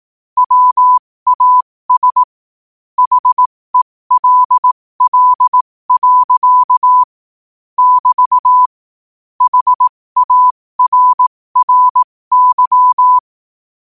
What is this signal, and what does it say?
Morse code, heard as a keyed tone, about 18 wpm, spaced as for 15 wpm, 1000 Hz. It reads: WAS HELL. <BT> HARRY
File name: qotdfiles / 15wpm